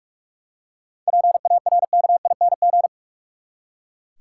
Morse code, heard as a keyed tone, about 45 wpm, 700 Hz. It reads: PARKING
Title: Seven Letter Words - 45wpm